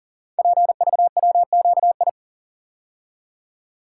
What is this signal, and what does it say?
Morse code, heard as a keyed tone, about 38 wpm, 700 Hz. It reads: PVWQI
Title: Code Groups - 5 Letters - 40wpm